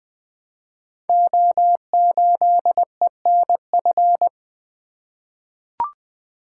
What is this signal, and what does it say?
Morse code, heard as a keyed tone, about 20 wpm, 700 Hz. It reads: O8ENF
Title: Code Groups - 5 Letter:Numbers - 20wpm